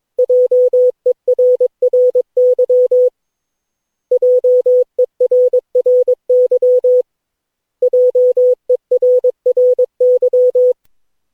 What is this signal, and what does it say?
Morse code, heard as a keyed tone, about 22 wpm, 500 Hz. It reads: JERRY JERRY JERRY
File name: cw / Session9 cw training